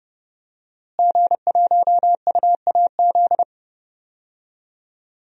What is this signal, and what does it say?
Morse code, heard as a keyed tone, about 32 wpm, 700 Hz. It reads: G1UAZ
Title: DXCC-100 Call Signs - 5 Characters - Repeated 2x - 30wpm